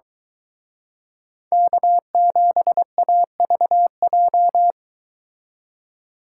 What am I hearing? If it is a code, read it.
Morse code, heard as a keyed tone, about 23 wpm, 700 Hz. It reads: K7AVJ